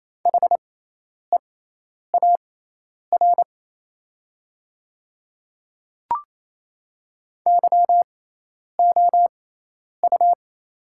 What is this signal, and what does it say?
Morse code, heard as a keyed tone, about 28 wpm, spaced as for 10 wpm, 700 Hz. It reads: HEAR YOU
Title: 6x Character Spacing - Top 500 Words - 28wpm